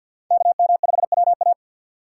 Morse code, heard as a keyed tone, about 52 wpm, 700 Hz. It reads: KM5WA
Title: USA Call Signs - 5 Characters - 4x Character Spacing - 50wpm